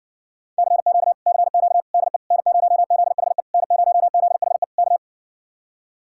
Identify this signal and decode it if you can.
Morse code, heard as a keyed tone, about 60 wpm, 700 Hz. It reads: CQ CQ DE N0ZHE N0ZHE K